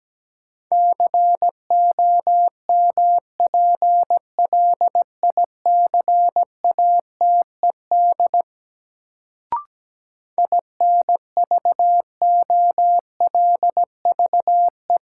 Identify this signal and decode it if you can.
Morse code, heard as a keyed tone, about 17 wpm, 700 Hz. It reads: COMPLICATED INVOLVE
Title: Sentences from Top 1200 Words - New Word Drill Alt - 17wpm